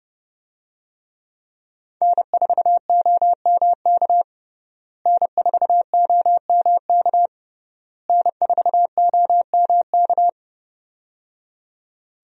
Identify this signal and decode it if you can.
Morse code, heard as a keyed tone, about 30 wpm, 700 Hz. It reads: N4OMK N4OMK N4OMK
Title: USA Call Signs - 5 Characters - Repeated 3x - 30wpm